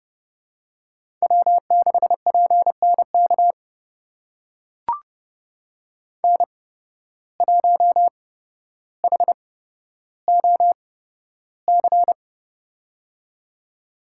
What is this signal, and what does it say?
Morse code, heard as a keyed tone, about 30 wpm, spaced as for 13 wpm, 700 Hz. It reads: W6PNK N1HOC
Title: USA Call Signs - 5 Characters - 8x Character Spacing - 30wpm